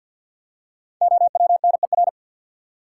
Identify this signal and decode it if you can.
Morse code, heard as a keyed tone, about 53 wpm, 700 Hz. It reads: OWNER
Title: Five Letter Words - 50wpm